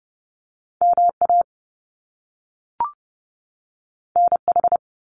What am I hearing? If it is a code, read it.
Morse code, heard as a keyed tone, about 30 wpm, 700 Hz. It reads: MA NH
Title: US States 30wpm